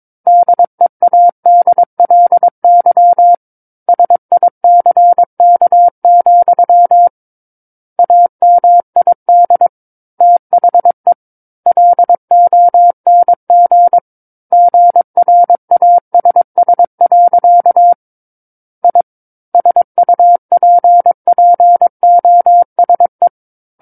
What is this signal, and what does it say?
Morse code, heard as a keyed tone, about 22 wpm, 700 Hz. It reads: DEADLY SICK, AMID THE LONG GRASS. I SUPPOSE
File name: Book One / 13-Chapter 13 - 22 WPM